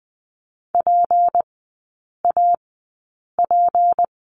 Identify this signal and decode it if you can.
Morse code, heard as a keyed tone, about 20 wpm, spaced as for 8 wpm, 700 Hz. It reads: PAP
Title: 152 Lesson 25e - Letter p - Three Characters 20wpm